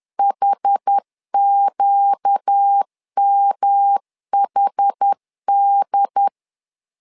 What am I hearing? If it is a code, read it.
Morse code, heard as a keyed tone, about 10 wpm, 800 Hz. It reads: HQMHD
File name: LEZIONI / 4 Q1 10wpm 2448